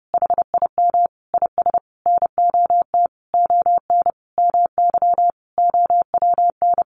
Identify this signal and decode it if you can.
Morse code, heard as a keyed tone, about 30 wpm, 700 Hz. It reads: HIM IS NOT ON MY OWN
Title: Sentences from Top 1400 Words 30wpm